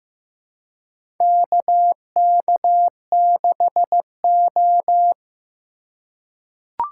Morse code, heard as a keyed tone, about 15 wpm, 700 Hz. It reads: KK6O
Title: USA Call Signs - 3x Character Spacing - 15wpm